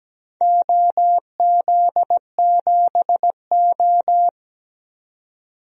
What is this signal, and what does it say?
Morse code, heard as a keyed tone, about 17 wpm, 700 Hz. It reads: OZ7O
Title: DXCC-100 Call Signs - 4 Characters - 17wpm